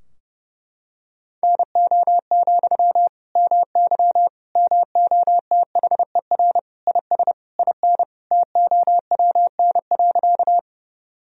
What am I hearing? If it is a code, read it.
Morse code, heard as a keyed tone, about 32 wpm, 700 Hz. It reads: NO, MY MOTHER IS IN TOWN.
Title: Sentences from Top 1000 Words - Review Part 2 - 30wpm